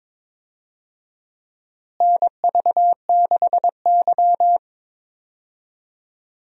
Morse code, heard as a keyed tone, about 22 wpm, 700 Hz. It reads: NV6Y
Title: USA Call Signs - 4 Characters - Repeated 3x - 22wpm